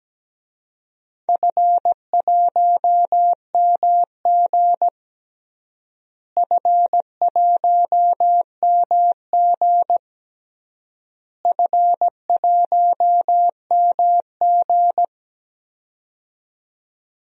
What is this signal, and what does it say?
Morse code, heard as a keyed tone, about 17 wpm, 700 Hz. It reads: F1MG F1MG F1MG
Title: DXCC-100 Call Signs - 4 Characters - Repeated 3x - 17wpm